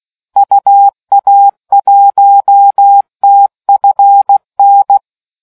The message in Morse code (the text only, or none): UA1TFN